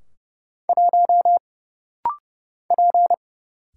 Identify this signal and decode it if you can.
Morse code, heard as a keyed tone, about 30 wpm, 700 Hz. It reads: 1 P
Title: Single Letter-Number - Rapid-Fire 30wpm